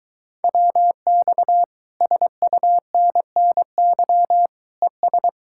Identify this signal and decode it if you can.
Morse code, heard as a keyed tone, about 23 wpm, 700 Hz. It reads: WX SUNNY ES